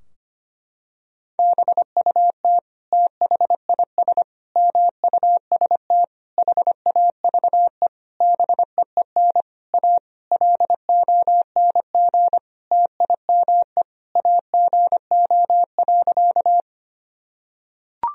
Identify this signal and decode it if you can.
Morse code, heard as a keyed tone, about 25 wpm, 700 Hz. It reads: BUT THIS MUST HAVE BEEN A LONG TIME AGO.
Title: Sentences from Top 1000 Words - Review Part 3 - 25wpm